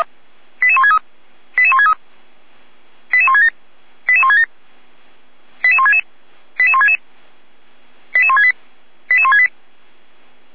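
Five-tone selective call (ZVEI), heard as quick runs of five tones, tones of about 70 ms. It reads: 88163 88163 88167 88167 88166 88166 88168 88168
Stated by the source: ZVEI five-tone selective call received by SDR.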